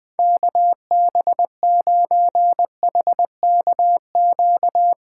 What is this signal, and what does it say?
Morse code, heard as a keyed tone, about 21 wpm, 700 Hz. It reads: KB9HKQ
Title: Speed Racing - USA 2x3 Call Signs 20wpm